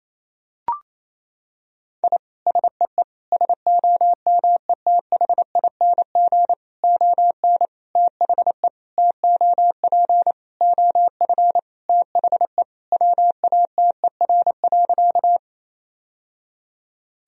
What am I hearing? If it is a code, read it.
Morse code, heard as a keyed tone, about 28 wpm, 700 Hz. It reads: I SEE SOMETHING ON THE TOP OF THE WATER.
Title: Sentences from Top 1000 Words - Review Part 2 - 28wpm